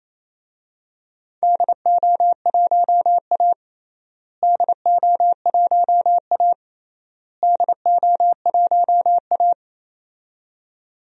Morse code, heard as a keyed tone, about 28 wpm, 700 Hz. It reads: DO1A DO1A DO1A
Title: DXCC-100 Call Signs - 4 Characters - Repeated 3x - 28wpm